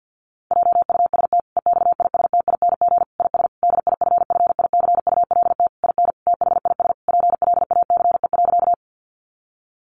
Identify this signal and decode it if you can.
Morse code, heard as a keyed tone, about 50 wpm, 700 Hz. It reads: JUST EXISTING IS DIFFICULT IN THIS PLACE.
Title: Sentences from Top 1500 Words 50wpm